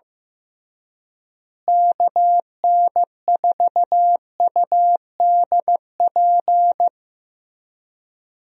Morse code, heard as a keyed tone, about 15 wpm, 700 Hz. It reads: KN4UDP
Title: USA Call Signs - 6 Characters - Repeated 3x - 15wpm